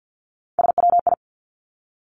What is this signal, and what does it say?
Morse code, heard as a keyed tone, about 50 wpm, 700 Hz. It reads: SWI
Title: Three Challenging Letter-Number Combinations 50wpm